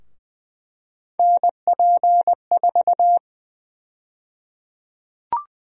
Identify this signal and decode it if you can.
Morse code, heard as a keyed tone, about 20 wpm, 700 Hz. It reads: NP4
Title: USA Call Sign Prefixes - 20wpm